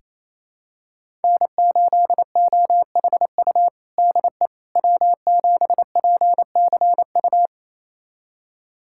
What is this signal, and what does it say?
Morse code, heard as a keyed tone, about 28 wpm, 700 Hz. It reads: N8OHU DE W7PCU
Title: USA Callsign DE Callsign - 5 Characters - 28wpm